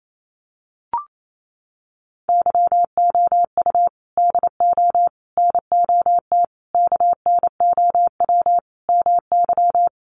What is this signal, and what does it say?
Morse code, heard as a keyed tone, about 28 wpm, 700 Hz. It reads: YOU DO NOT KNOW MY
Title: Sentences from Top 600 Words 28wpm